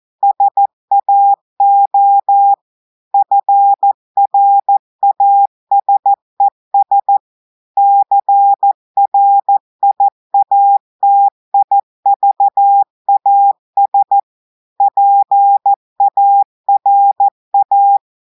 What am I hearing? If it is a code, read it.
Morse code, heard as a keyed tone, about 14 wpm, 800 Hz. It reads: SAO FRASES CRIATIVAS PARA
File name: audios / portugais / portugais texte en clair 2 vitesse 14 mots minutes